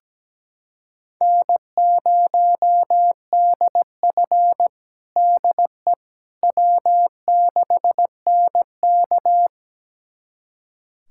Morse code, heard as a keyed tone, about 17 wpm, 700 Hz. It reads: N0DF DE W6NK
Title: USA Callsign DE Callsign - 4 Characters - 17wpm